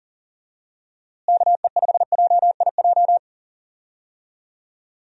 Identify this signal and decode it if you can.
Morse code, heard as a keyed tone, about 40 wpm, 700 Hz. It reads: KE5JIJ
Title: USA Call Signs - 6 Characters - Repeated 3x - 40wpm